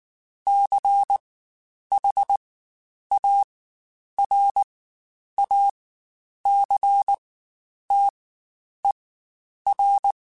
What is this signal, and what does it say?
Morse code, heard as a keyed tone, about 19 wpm, spaced as for 9 wpm, 800 Hz. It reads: CHARACTER